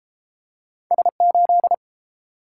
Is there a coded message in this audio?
Yes: Morse code from 0.9 to 1.7 s, heard as a keyed tone, about 30 wpm, 700 Hz. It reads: S8